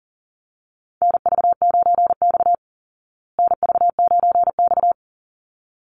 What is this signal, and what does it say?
Morse code, heard as a keyed tone, about 40 wpm, 700 Hz. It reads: NV9X NV9X
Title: Call Signs Repeated 3x - 40wpm